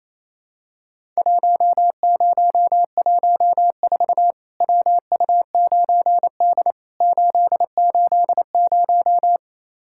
Morse code, heard as a keyed tone, about 28 wpm, 700 Hz. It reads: 1014 WU9D 880
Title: Worked All Europe DX Contest - QTC Exchange - 28wpm